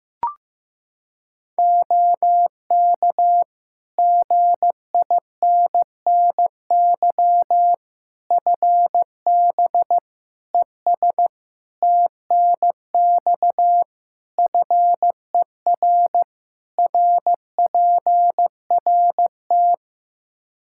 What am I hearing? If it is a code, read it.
Morse code, heard as a keyed tone, about 15 wpm, 700 Hz. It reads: OK GINNY FB ES TNX FER RPRT
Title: Practice QSO - Exchange 2 - Info - 15wpm